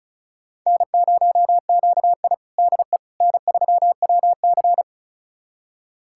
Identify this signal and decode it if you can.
Morse code, heard as a keyed tone, about 36 wpm, 700 Hz. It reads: N0QI DE N3WC
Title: USA Callsign DE Callsign - 4 Characters - 35wpm